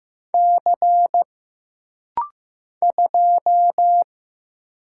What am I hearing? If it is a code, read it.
Morse code, heard as a keyed tone, about 15 wpm, 700 Hz. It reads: C 2
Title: Single Letter-Number Alt - Rapid-Fire 15wpm